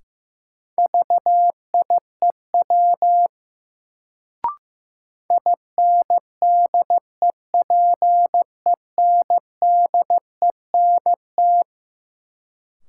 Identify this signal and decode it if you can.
Morse code, heard as a keyed tone, about 15 wpm, 700 Hz. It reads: VIEW INDEPENDENT
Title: Sentences from Top 900 Words - New Word Drill - 15wpm